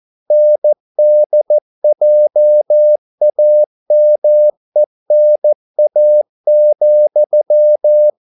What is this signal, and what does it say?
Morse code, heard as a keyed tone, about 14 wpm, 600 Hz. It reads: NDJAMENA,